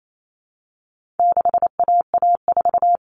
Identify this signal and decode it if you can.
Morse code, heard as a keyed tone, about 28 wpm, 700 Hz. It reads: 6AA4
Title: All Cut Numbers - 4-Digit 28wpm